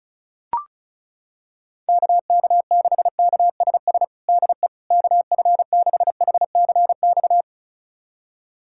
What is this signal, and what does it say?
Morse code, heard as a keyed tone, about 36 wpm, 690 Hz. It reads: KK6KSS DE KF6HCX
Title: USA Callsign DE Callsign - 6 Characters - 35wpm